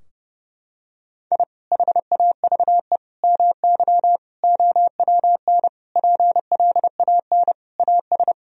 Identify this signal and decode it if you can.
Morse code, heard as a keyed tone, about 30 wpm, 700 Hz. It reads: I HAVE MY OWN PLAN AS
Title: Sentences from Top 1000 Words - Review Part 2 - 30wpm